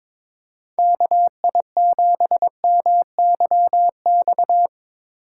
Morse code, heard as a keyed tone, about 23 wpm, 700 Hz. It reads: KI7MYX
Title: USA Call Signs - 6 Characters - 2x Character Spacing - 22wpm